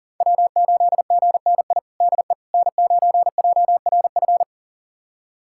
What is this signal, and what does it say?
Morse code, heard as a keyed tone, about 40 wpm, 700 Hz. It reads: W8GNI DE N9JRF
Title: USA Callsign DE Callsign - 5 Characters - 40wpm